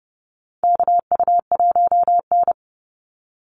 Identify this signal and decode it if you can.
Morse code, heard as a keyed tone, about 30 wpm, 700 Hz. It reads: KU1N
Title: Call Signs Repeated 3x - 30wpm